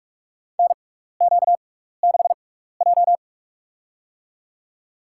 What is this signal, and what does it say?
Morse code, heard as a keyed tone, about 42 wpm, spaced as for 15 wpm, 700 Hz. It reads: NQ6J
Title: USA Call Signs - 4 Characters - 6x Character Spacing - 45wpm